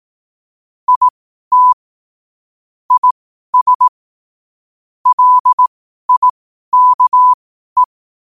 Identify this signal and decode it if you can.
Morse code, heard as a keyed tone, about 18 wpm, spaced as for 12 wpm, 1000 Hz. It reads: IT IS LIKE
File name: qotdfiles / 12wpm